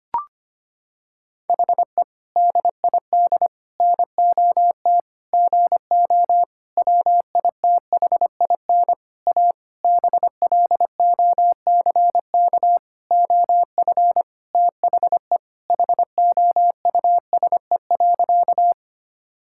Morse code, heard as a keyed tone, about 25 wpm, 700 Hz. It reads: HE DID NOT GO WITHIN A BLOCK OF THE HOUSE.
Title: Sentences from Top 1600 Words 25wpm